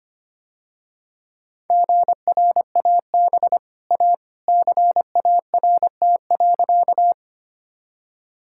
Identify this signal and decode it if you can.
Morse code, heard as a keyed tone, about 25 wpm, 700 Hz. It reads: GRAB A CART.
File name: Easy Sentences with Hints - 25wpm